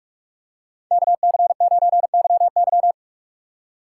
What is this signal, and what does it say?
Morse code, heard as a keyed tone, about 46 wpm, 700 Hz. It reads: KC9YY